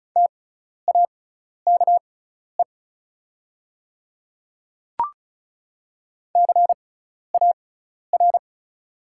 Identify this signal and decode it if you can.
Morse code, heard as a keyed tone, about 36 wpm, spaced as for 12 wpm, 690 Hz. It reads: TAKE CAR
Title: 6x Character Spacing - Top 300 Words - 35wpm